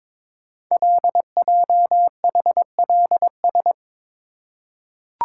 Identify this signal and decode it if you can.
Morse code, heard as a keyed tone, about 22 wpm, 700 Hz. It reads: LJHLS
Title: Code Groups - 5 Letters - 22wpm